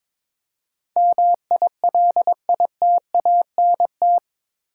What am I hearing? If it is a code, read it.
Morse code, heard as a keyed tone, about 22 wpm, 700 Hz. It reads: MILITANT
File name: practice / 22wpm / Eight Letter Words - 22wpm